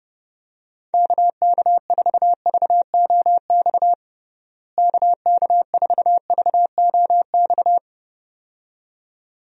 Morse code, heard as a keyed tone, about 32 wpm, 700 Hz. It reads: KK4VOX KK4VOX
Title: USA Call Signs - 6 Characters - Repeated 2x - 30wpm